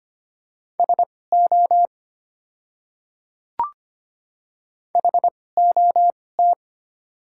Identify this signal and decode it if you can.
Morse code, heard as a keyed tone, about 25 wpm, spaced as for 18 wpm, 690 Hz. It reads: SO HOT
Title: 2x Character Spacing - Top 100 Words - 25wpm